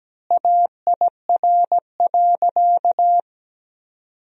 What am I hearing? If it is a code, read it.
Morse code, heard as a keyed tone, about 17 wpm, 700 Hz. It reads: AIR.
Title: 4x Word-Spacing - Sentences from Top 200 Words - 17wpm